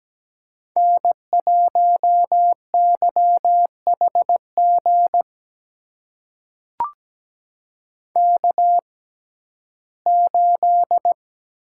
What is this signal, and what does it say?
Morse code, heard as a keyed tone, about 17 wpm, spaced as for 12 wpm, 700 Hz. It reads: N1YHG K8
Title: USA Call Signs - 5 Characters - 6x Character Spacing - 17wpm